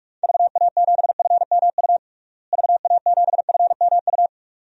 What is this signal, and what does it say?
Morse code, heard as a keyed tone, about 46 wpm, 700 Hz. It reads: VA7FMU VA7FMU
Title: DXCC-100 Call Signs - 6 Characters - Repeated 2x - 45wpm